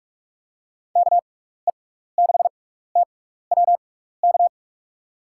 Morse code, heard as a keyed tone, about 47 wpm, spaced as for 16 wpm, 700 Hz. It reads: KE6TWK
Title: USA Call Signs - 6x Character Spacing - 45wpm